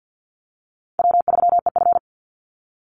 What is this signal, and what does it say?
Morse code, heard as a keyed tone, about 50 wpm, 700 Hz. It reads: W3EF
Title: Easy Call Signs - 50wpm